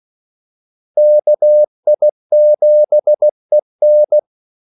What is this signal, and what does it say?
Morse code, heard as a keyed tone, about 16 wpm, 600 Hz. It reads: KI7EN
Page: Kaldesignaler 16 wpm